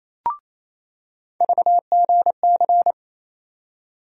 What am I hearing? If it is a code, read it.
Morse code, heard as a keyed tone, about 28 wpm, 700 Hz. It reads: VGC
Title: USA Call Sign Postfixes - 3 Letters 28wpm